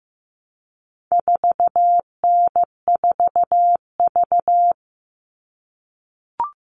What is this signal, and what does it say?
Morse code, heard as a keyed tone, about 15 wpm, 700 Hz. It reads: 4N4V
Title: All Cut Numbers - 4-Digit 15wpm